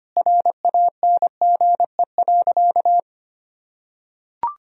Morse code, heard as a keyed tone, about 25 wpm, 700 Hz. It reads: RANGE.
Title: Sentences from Top 1000 Words - Review Part 4 - 25wpm